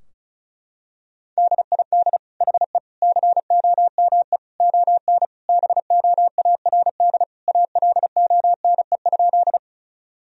Morse code, heard as a keyed tone, about 34 wpm, 700 Hz. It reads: DID HE COME ON BOARD ALONE?
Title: Sentences from Top 1000 Words - Review Part 4 - 35wpm